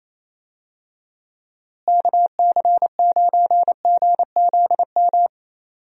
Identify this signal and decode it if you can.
Morse code, heard as a keyed tone, about 27 wpm, 700 Hz. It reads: KC9GZM
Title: USA Call Signs - 6 Characters - Repeated 3x - 28wpm